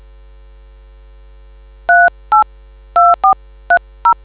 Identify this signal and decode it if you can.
DTMF, heard as telephone touch tones, tones of about 120 ms. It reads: 382430